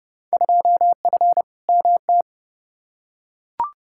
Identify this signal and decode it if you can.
Morse code, heard as a keyed tone, about 30 wpm, 700 Hz. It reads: 2F MT
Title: Field Day Exchanges Encore - 30wpm